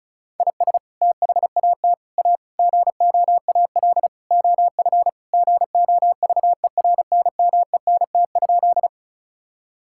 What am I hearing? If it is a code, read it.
Morse code, heard as a keyed tone, about 34 wpm, 700 Hz. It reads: IS THAT A GOAL OF GOVERNMENT?
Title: Sentences from Top 1000 Words - Review Part 5 - 35wpm